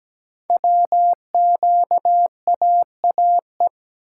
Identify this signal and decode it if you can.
Morse code, heard as a keyed tone, about 17 wpm, 700 Hz. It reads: WQAAE
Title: Code Groups - 5 Letter:Numbers - 17wpm